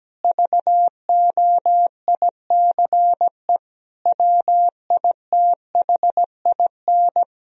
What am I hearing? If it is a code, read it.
Morse code, heard as a keyed tone, about 17 wpm, 700 Hz. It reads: VOICE WITHIN
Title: Sentences from Top 1000 Words - Review Part 4 - 17wpm